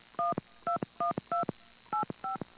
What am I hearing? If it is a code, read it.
DTMF, heard as telephone touch tones, tones of about 130 ms. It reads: 121285